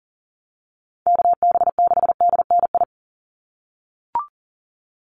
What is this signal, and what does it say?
Morse code, heard as a keyed tone, about 40 wpm, 700 Hz. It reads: KB6DNI
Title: USA Call Signs 40wpm